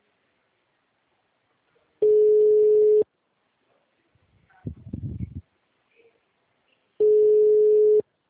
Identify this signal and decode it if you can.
Ringback tone (Europe, 425 Hz).